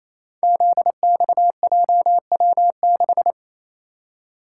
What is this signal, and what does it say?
Morse code, heard as a keyed tone, about 28 wpm, 700 Hz. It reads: ZXJW6
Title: Code Groups - 5 Letter:Numbers - 28wpm